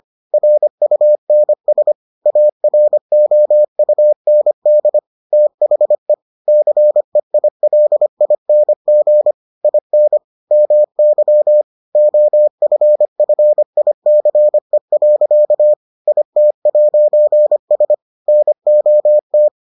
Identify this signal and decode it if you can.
Morse code, heard as a keyed tone, about 25 wpm, 600 Hz. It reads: RUNS AROUND THE CEILING IN MY OFFICE. IT'S NOT